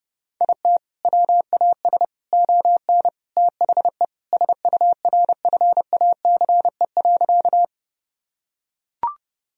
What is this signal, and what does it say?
Morse code, heard as a keyed tone, about 32 wpm, 700 Hz. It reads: IT WAS ON THE SURFACE.
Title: Sentences from Top 1000 Words - Review Part 3 - 30wpm